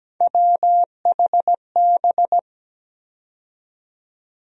Morse code, heard as a keyed tone, about 17 wpm, 700 Hz. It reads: WHB
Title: USA Call Sign Postfixes - 3 Letters 17wpm